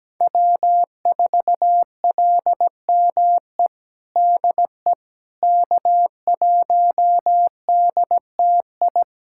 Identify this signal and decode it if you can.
Morse code, heard as a keyed tone, about 17 wpm, 700 Hz. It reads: W4LME DE K1DTI
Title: USA Callsign DE Callsign - 5 Characters - 17wpm